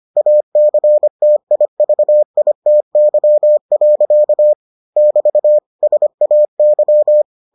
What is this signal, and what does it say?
Morse code, heard as a keyed tone, about 25 wpm, 600 Hz. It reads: ACTIVITY. <BT> SAY